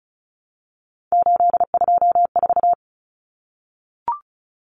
Morse code, heard as a keyed tone, about 34 wpm, 700 Hz. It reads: 824
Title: Common Cut Numbers - 3-Digit 35wpm